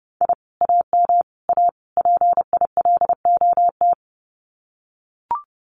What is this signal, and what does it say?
Morse code, heard as a keyed tone, about 30 wpm, 700 Hz. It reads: I AM A PILOT
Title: Common QSO Phrases Encore 30wpm